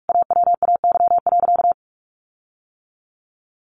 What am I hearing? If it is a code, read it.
Morse code, heard as a keyed tone, about 44 wpm, 700 Hz. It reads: AWAY.
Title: Sentences from Top 1500 Words 45wpm